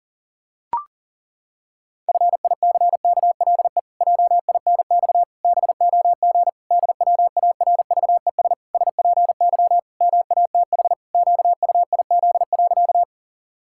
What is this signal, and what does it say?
Morse code, heard as a keyed tone, about 37 wpm, 700 Hz. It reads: FICKLE JINX BOG DWARVES SPY MATH QUIZ.
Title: Pangram Challenge v2 - 40wpm